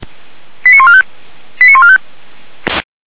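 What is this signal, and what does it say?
Five-tone selective call (ZVEI), heard as quick runs of five tones, tones of about 70 ms. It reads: 89145 89145
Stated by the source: ZVEI five-tone selective call received on an SDR.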